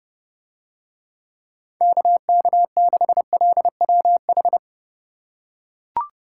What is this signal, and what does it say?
Morse code, heard as a keyed tone, about 27 wpm, 700 Hz. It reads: KK6LWH